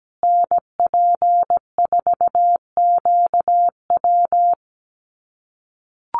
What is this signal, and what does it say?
Morse code, heard as a keyed tone, about 17 wpm, 700 Hz. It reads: NP4QW
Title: USA Call Signs Repeated 3x 17wpm